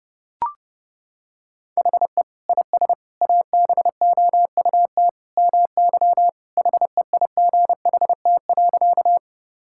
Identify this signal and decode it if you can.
Morse code, heard as a keyed tone, about 30 wpm, 700 Hz. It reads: HE IS ABOUT MY HEIGHT.
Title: Sentences from Top 1600 Words 30wpm